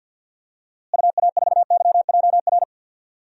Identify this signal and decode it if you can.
Morse code, heard as a keyed tone, about 52 wpm, 700 Hz. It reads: UA4YJR
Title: DXCC-100 Call Signs - 6 Characters - Repeated 3x - 50wpm